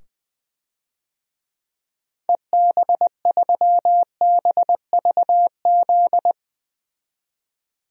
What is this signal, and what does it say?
Morse code, heard as a keyed tone, about 20 wpm, 700 Hz. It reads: EB3BVZ
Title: DXCC-100 Call Signs - 6 Characters - 20wpm